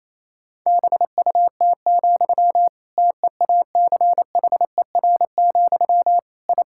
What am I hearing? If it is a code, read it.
Morse code, heard as a keyed tone, about 28 wpm, 700 Hz. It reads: BUT, TEACHER, I
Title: Sentences from Top 1000 Words - Review Part 6 - 28wpm